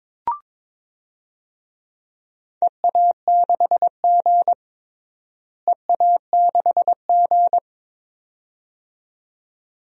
Morse code, heard as a keyed tone, about 22 wpm, 700 Hz. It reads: EA6G EA6G
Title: DXCC-100 Call Signs - 4 Characters - Repeated 2x - 22wpm